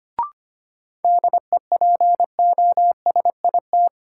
Morse code, heard as a keyed tone, about 25 wpm, 700 Hz. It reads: DEPOSIT
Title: Sentences from Top 1300 Words - New Word Drill - 25wpm